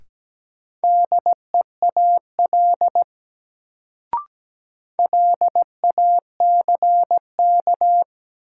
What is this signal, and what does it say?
Morse code, heard as a keyed tone, about 17 wpm, 700 Hz. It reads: DEAL LACK
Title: Sets of 3 Words - New Word Drill - 17wpm